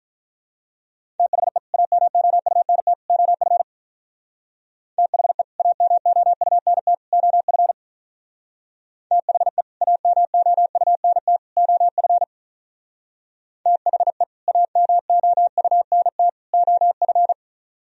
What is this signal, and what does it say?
Morse code, heard as a keyed tone, about 43 wpm, 700 Hz. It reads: THE AMOUNT OF THE AMOUNT OF THE AMOUNT OF THE AMOUNT OF
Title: Speed-Racing - Sets of 3 Words 35wpm